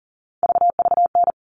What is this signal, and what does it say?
Morse code, heard as a keyed tone, about 40 wpm, 700 Hz. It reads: VVN
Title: All Cut Numbers - 3-Digit 40wpm